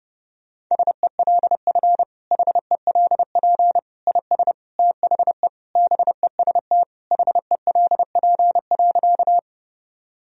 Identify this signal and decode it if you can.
Morse code, heard as a keyed tone, about 30 wpm, 700 Hz. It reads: SELF HELP IS THE BEST HELP.
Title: Sentences from Top 1000 Words - Review Part 1 - 30wpm